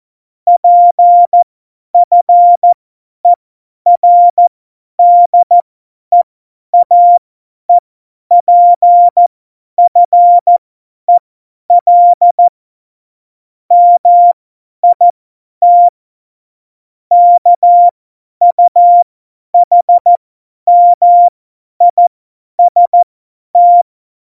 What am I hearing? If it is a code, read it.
Morse code, heard as a keyed tone, about 14 wpm, spaced as for 10 wpm, 700 Hz. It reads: PFERDEAEPFEL MIT KUHMIST
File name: Morsekurs / Witze / Witz 03.Dezember 2019, 50 F70